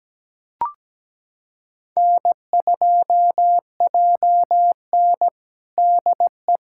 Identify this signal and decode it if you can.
Morse code, heard as a keyed tone, about 17 wpm, 700 Hz. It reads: N2JN DE
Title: USA Callsign DE Callsign - 4 Characters - 17wpm